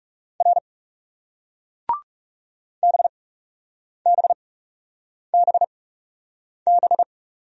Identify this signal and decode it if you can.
Morse code, heard as a keyed tone, about 40 wpm, spaced as for 8 wpm, 700 Hz. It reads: R BBBB